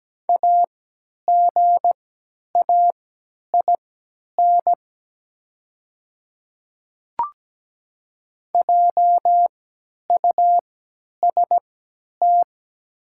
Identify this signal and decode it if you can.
Morse code, heard as a keyed tone, about 17 wpm, spaced as for 10 wpm, 700 Hz. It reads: AGAIN JUST
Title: 3x Character Spacing - Top 200 Words - 17wpm